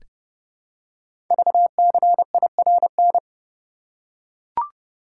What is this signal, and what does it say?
Morse code, heard as a keyed tone, about 30 wpm, 700 Hz. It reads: VCIRN
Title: Code Groups - 5 Letters - 30wpm